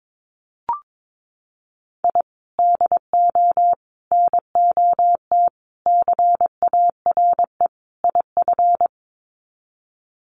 Morse code, heard as a keyed tone, about 22 wpm, 700 Hz. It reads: I DO NOT CARE IF
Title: Sets of 5 Words 22wpm